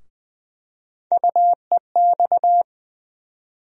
Morse code, heard as a keyed tone, about 20 wpm, 700 Hz. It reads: UEX
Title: USA Call Sign Postfixes - 3 Letters 20wpm